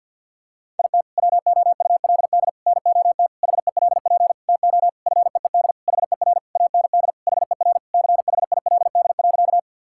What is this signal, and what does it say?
Morse code, heard as a keyed tone, about 50 wpm, 700 Hz. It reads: IT WOULD NOT HELP TO FEED HER AND HER CHILD.
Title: Sentences from Top 1600 Words 50wpm